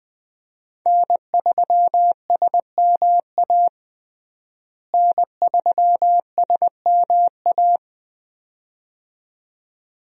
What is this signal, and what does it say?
Morse code, heard as a keyed tone, about 20 wpm, 700 Hz. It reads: N3SMA N3SMA
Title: USA Call Signs - 5 Characters - Repeated 2x - 20wpm